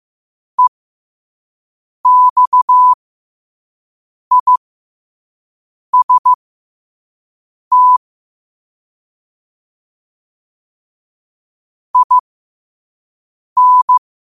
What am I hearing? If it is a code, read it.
Morse code, heard as a keyed tone, about 15 wpm, spaced as for 5 wpm, 1000 Hz. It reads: EXIST IN
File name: qotdfiles / 5wpm